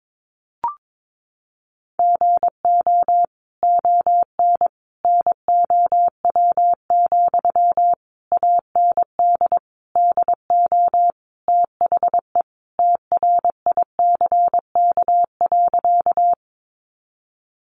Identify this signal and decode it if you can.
Morse code, heard as a keyed tone, about 22 wpm, 700 Hz. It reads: GO ON NOW, AND DO THE TRICK.
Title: Sentences from Top 1500 Words 22wpm